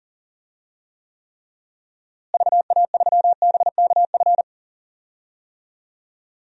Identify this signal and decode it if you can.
Morse code, heard as a keyed tone, about 40 wpm, 700 Hz. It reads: VA3BKF
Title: DXCC-100 Call Signs - 6 Characters - Repeated 3x - 40wpm